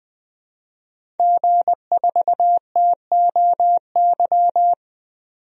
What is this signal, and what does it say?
Morse code, heard as a keyed tone, about 20 wpm, 700 Hz. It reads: G4TOY
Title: DXCC-100 Call Signs - 5 Characters - 20wpm